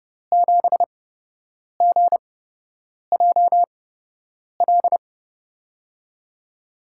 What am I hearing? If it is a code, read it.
Morse code, heard as a keyed tone, about 30 wpm, spaced as for 8 wpm, 700 Hz. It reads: 7GJL
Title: USA Call Signs - 5 Characters - 8x Character Spacing - 30wpm